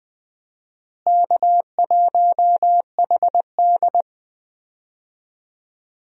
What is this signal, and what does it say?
Morse code, heard as a keyed tone, about 20 wpm, 700 Hz. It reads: K1HD
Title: USA Call Signs - 4 Characters - Repeated 3x - 20wpm